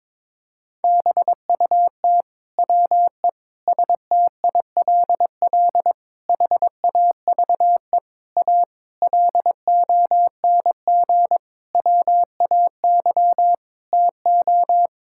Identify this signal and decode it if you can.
Morse code, heard as a keyed tone, about 22 wpm, 700 Hz. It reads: BUT WE STILL HAVE A LONG WAY TO
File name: Sentences with 9-Word Phrases - 22wpm